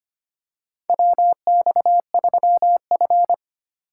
Morse code, heard as a keyed tone, about 25 wpm, 700 Hz. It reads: WX3F
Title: USA Call Signs - 4 Characters - 4x Character Spacing - 25wpm